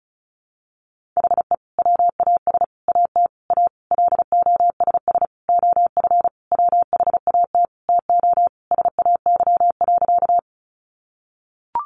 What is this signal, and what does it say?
Morse code, heard as a keyed tone, about 34 wpm, 700 Hz. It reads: HE WAS AT A LOSS OF WHAT TO SAY.
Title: Sentences from Top 1400 Words 35wpm